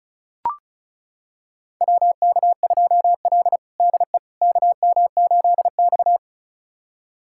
Morse code, heard as a keyed tone, about 34 wpm, 700 Hz. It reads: WK2L DE KM8X
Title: USA Callsign DE Callsign - 4 Characters - 35wpm